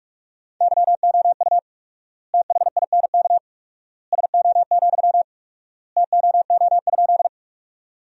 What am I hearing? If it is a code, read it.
Morse code, heard as a keyed tone, about 41 wpm, 700 Hz. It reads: YOU THINK SO, TOO?
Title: 4x Word-Spacing - Sentences from Top 200 Words - 45wpm